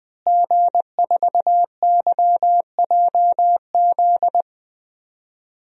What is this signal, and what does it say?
Morse code, heard as a keyed tone, about 20 wpm, 700 Hz. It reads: G4YJZ
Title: DXCC-100 Call Signs - 5 Characters - Repeated 3x - 20wpm